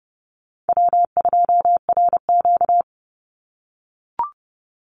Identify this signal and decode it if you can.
Morse code, heard as a keyed tone, about 30 wpm, 700 Hz. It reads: W2RQ
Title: Easy Call Signs - 30wpm